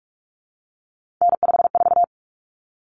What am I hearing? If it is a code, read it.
Morse code, heard as a keyed tone, about 44 wpm, 700 Hz. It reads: N54
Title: Common Cut Numbers - 3-Digit 45wpm